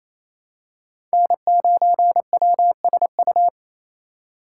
Morse code, heard as a keyed tone, about 28 wpm, 700 Hz. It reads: N9WSU
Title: USA Call Signs - 5 Characters - 4x Character Spacing - 28wpm